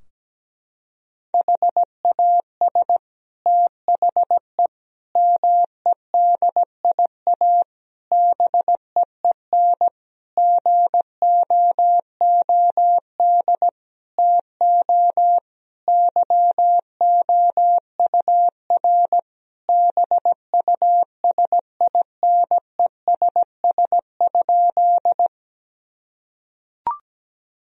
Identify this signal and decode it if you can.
Morse code, heard as a keyed tone, about 17 wpm, 700 Hz. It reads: HAS THE MEDIA BEEN GOOD TO YOUR BUSINESS?
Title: Sentences from Top 1000 Words - Review Part 4 - 17wpm